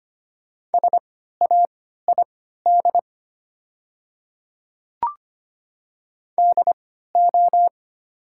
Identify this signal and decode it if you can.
Morse code, heard as a keyed tone, about 25 wpm, spaced as for 14 wpm, 700 Hz. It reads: SAID DO
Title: 3x Character Spacing - Top 500 Words - 25wpm